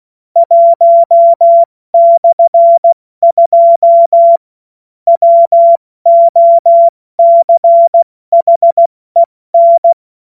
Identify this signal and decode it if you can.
Morse code, heard as a keyed tone, about 16 wpm, 680 Hz. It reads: 1/2 WOCHEN
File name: Morsekurs / Witze / Witz 02 April 2024 70 F80